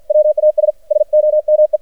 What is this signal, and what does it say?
Morse code, heard as a keyed tone, about 52 wpm, 610 Hz. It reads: WAS SOME
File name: archived / CRSnet